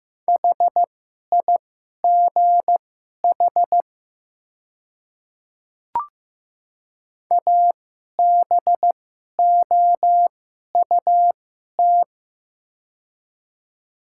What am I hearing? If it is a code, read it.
Morse code, heard as a keyed tone, about 15 wpm, spaced as for 11 wpm, 700 Hz. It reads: HIGH ABOUT
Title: 2x Character Spacing - Top 200 Words - 15wpm